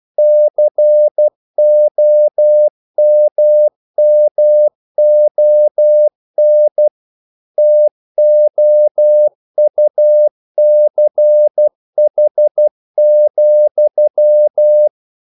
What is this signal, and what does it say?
Morse code, heard as a keyed tone, about 12 wpm, 600 Hz. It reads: COMMON TOUCH,
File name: CW / Practice / If12